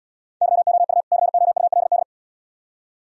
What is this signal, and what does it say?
Morse code, heard as a keyed tone, about 78 wpm, 700 Hz. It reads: YOU COULD